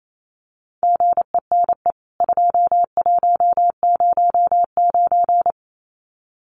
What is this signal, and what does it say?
Morse code, heard as a keyed tone, about 28 wpm, 700 Hz. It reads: GENE 2109
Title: CWT Exchange 28wpm